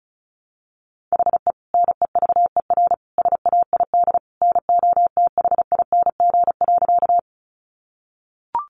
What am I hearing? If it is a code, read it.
Morse code, heard as a keyed tone, about 34 wpm, 700 Hz. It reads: HE NEVER SAID NOTHING.
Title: Sentences from Top 500 Words 35wpm